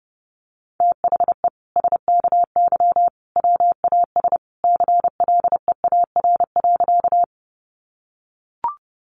Morse code, heard as a keyed tone, about 30 wpm, 700 Hz. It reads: THE SKY WAS CLEAR.
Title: Sentences from Top 1200 Words 30wpm